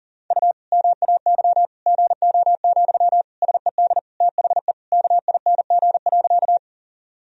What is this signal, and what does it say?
Morse code, heard as a keyed tone, about 37 wpm, 700 Hz. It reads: U MAY GO, SED THE KING.
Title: Sentences from Top 400 Words and Abbreviations 40wpm